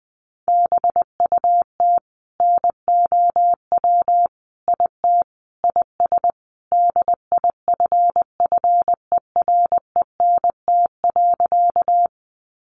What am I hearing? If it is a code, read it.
Morse code, heard as a keyed tone, about 20 wpm, 700 Hz. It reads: BUT NOW IT IS DIFFERENT.
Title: Sentences from Top 600 Words 20wpm